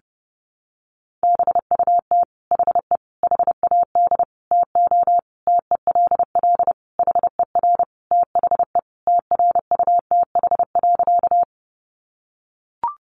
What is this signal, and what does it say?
Morse code, heard as a keyed tone, about 30 wpm, 700 Hz. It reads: BUT HE HAD TO TELL HER THE TRUTH.
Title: Sentences from Top 800 Words 30wpm